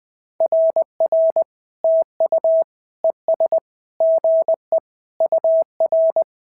Morse code, heard as a keyed tone, about 20 wpm, 650 Hz. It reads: RR TU ES GE UR
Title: SOTA QSO Practice 20wpm